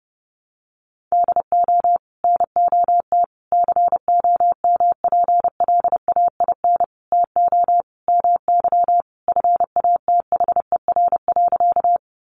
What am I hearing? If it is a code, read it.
Morse code, heard as a keyed tone, about 28 wpm, 700 Hz. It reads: DO NOT COMPLAIN TO MY FATHER.
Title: Sentences from Top 1300 Words 30wpm